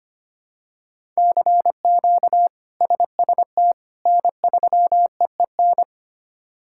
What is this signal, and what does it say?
Morse code, heard as a keyed tone, about 25 wpm, 700 Hz. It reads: CQ SST N3EEN
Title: K1USN SST QSO Practice - 25wpm